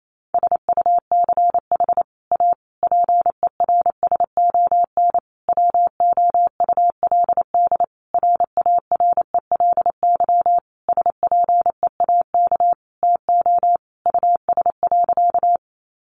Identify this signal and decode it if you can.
Morse code, heard as a keyed tone, about 28 wpm, 700 Hz. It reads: SUCH A PERSON WOULD RARELY SPEAK TO US.
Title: Sentences from Top 1400 Words 28wpm